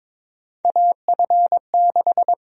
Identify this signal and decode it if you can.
Morse code, heard as a keyed tone, about 22 wpm, 700 Hz. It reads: AF6
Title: USA Call Sign Prefixes - Repeated 2x - 22wpm